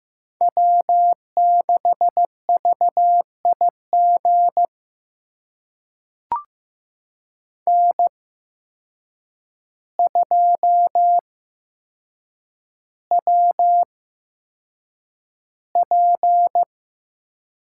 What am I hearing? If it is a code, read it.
Morse code, heard as a keyed tone, about 15 wpm, spaced as for 7 wpm, 700 Hz. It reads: W6VIG N2WP
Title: USA Call Signs - 8x Character Spacing - 15wpm